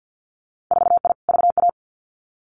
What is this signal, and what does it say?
Morse code, heard as a keyed tone, about 50 wpm, 700 Hz. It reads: 4I VA
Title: Winter Field Day Exchanges 50wpm